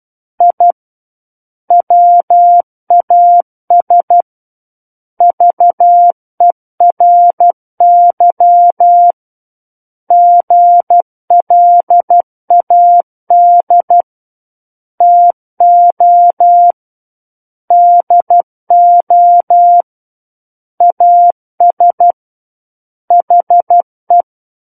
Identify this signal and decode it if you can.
Morse code, heard as a keyed tone, about 12 wpm, 700 Hz. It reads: I WAS VERY GLAD TO DO AS HE
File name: Book One / 03-Chapter 3 - 12 WPM